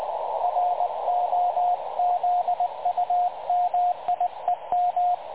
Morse code, heard as a keyed tone, about 22 wpm, 750 Hz, with radio noise. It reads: ROZUMIEM
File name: dlazaawansowanych / nagraniazpasma / websdr recording 3565.0kHz